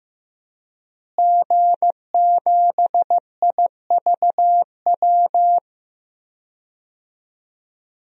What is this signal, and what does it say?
Morse code, heard as a keyed tone, about 15 wpm, 700 Hz. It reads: G7IVW
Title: DXCC-100 Call Signs - 5 Characters - Repeated 3x - 15wpm